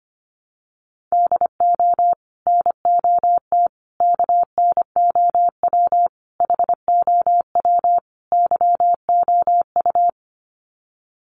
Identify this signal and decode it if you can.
Morse code, heard as a keyed tone, about 25 wpm, 700 Hz. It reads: DO NOT KNOW HOW YOU
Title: Sets of 5 Words Encore 25wpm